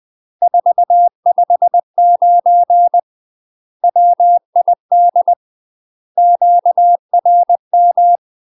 Morse code, heard as a keyed tone, about 20 wpm, 700 Hz. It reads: 459 WID QRM
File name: UK MASTER QSOs TRAINING/20WPM